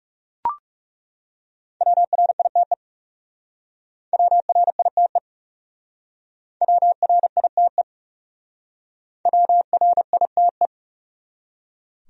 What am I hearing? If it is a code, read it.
Morse code, heard as a keyed tone, about 36 wpm, 700 Hz. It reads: WRITE WRITE WRITE WRITE
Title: Speed-Racing Top 500 Words 30wpm